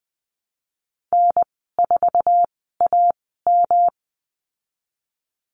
Morse code, heard as a keyed tone, about 20 wpm, spaced as for 15 wpm, 700 Hz. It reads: N4AM
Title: 140 Lesson 23g - Letter f - Call Signs 20wpm